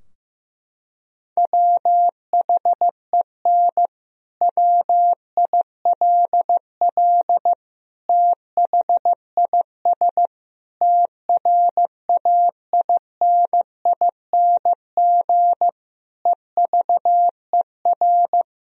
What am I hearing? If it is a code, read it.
Morse code, heard as a keyed tone, about 15 wpm, 700 Hz. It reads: WHEN WILL THIS TRAINING EVER
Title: Sentences from Top 1000 Words - Review Part 4 - 15wpm